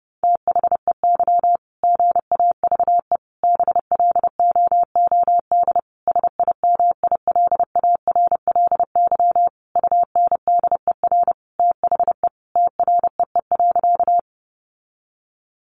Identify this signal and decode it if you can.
Morse code, heard as a keyed tone, about 30 wpm, 700 Hz. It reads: THEY GAVE BLOOD SIMILARLY UNDER THE TREE.
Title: Sentences from Top 1400 Words 30wpm